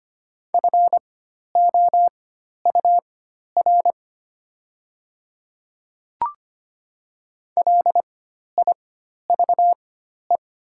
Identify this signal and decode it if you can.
Morse code, heard as a keyed tone, about 25 wpm, spaced as for 12 wpm, 700 Hz. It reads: FOUR LIVE
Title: 4x Character Spacing - Top 300 Words - 25wpm